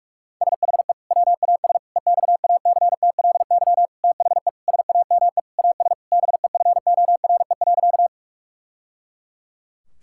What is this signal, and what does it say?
Morse code, heard as a keyed tone, about 41 wpm, 700 Hz. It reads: SHE WAS EXACTLY THE SAME AS BEFORE.
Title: Sentences from Top 1000 Words - Review Part 6 - 45wpm